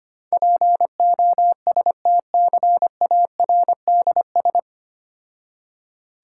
Morse code, heard as a keyed tone, about 25 wpm, 700 Hz. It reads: POSTCARDS
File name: practice / 25wpm / Difficult Words with Hints - 25wpm